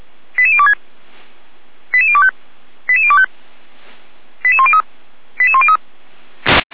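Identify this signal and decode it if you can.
Five-tone selective call (ZVEI), heard as quick runs of five tones, tones of about 70 ms. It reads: 88026 88025 88025 88103 88103
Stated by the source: ZVEI five-tone call 88102 received on SDR, Feuerwehr Orlamünde.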